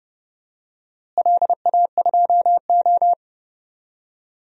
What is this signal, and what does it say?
Morse code, heard as a keyed tone, about 30 wpm, 700 Hz. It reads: LA2O
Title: DXCC-100 Call Signs - 4 Characters - Repeated 3x - 30wpm